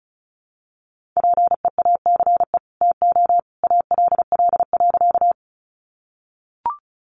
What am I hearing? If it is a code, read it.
Morse code, heard as a keyed tone, about 34 wpm, 700 Hz. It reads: PEACE TO ALL.
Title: Sentences from Top 1500 Words 35wpm